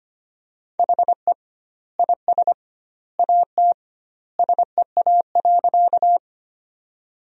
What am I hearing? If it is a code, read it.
Morse code, heard as a keyed tone, about 25 wpm, 700 Hz. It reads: HE IS AT SEA.
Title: 2x Word-Spacing - Sentences from Top 300 Words - 25wpm